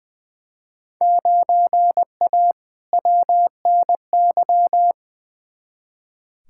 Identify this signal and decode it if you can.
Morse code, heard as a keyed tone, about 21 wpm, 700 Hz. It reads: 9A WNY
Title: Field Day Exchanges Encore - 20wpm